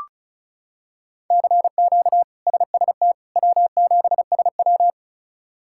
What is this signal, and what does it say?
Morse code, heard as a keyed tone, about 34 wpm, 700 Hz. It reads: CQ SST W7SW
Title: K1USN SST QSO Practice - 35wpm